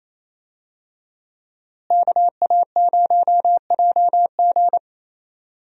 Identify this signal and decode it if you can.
Morse code, heard as a keyed tone, about 28 wpm, 700 Hz. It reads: KA0JG
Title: USA Call Signs - 5 Characters - Repeated 3x - 28wpm